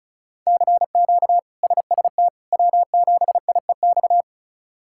Morse code, heard as a keyed tone, about 34 wpm, 700 Hz. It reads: CQ SST W7IEX
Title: K1USN SST QSO Practice - 35wpm